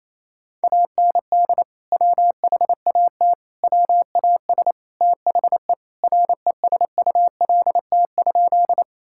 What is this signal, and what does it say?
Morse code, heard as a keyed tone, about 28 wpm, 700 Hz. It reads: AND WHAT WAS THE RESULT?
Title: Sentences from Top 1000 Words - Review Part 5 - 28wpm